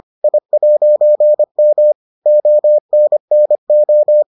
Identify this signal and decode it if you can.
Morse code, heard as a keyed tone, about 25 wpm, 600 Hz. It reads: I'M ONNO